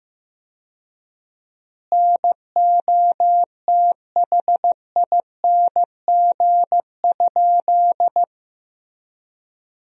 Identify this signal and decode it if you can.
Morse code, heard as a keyed tone, about 15 wpm, 700 Hz. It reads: NOTHING?
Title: 4x Word-Spacing - Sentences from Top 500 Words - 15wpm